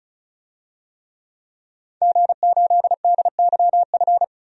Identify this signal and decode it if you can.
Morse code, heard as a keyed tone, about 34 wpm, 700 Hz. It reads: G8DYF
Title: DXCC-100 Call Signs - 5 Characters - 35wpm